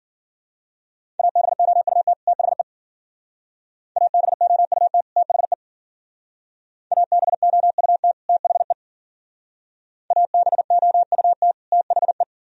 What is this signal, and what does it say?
Morse code, heard as a keyed tone, about 48 wpm, 700 Hz. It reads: ABOUT THE ABOUT THE ABOUT THE ABOUT THE
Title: Speed-Racing - Sets of 2 Words 40wpm